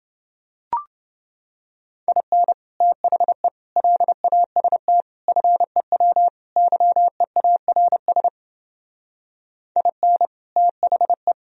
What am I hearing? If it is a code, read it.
Morse code, heard as a keyed tone, about 29 wpm, 700 Hz. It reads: IN THE LAST FEW YEARS IN THE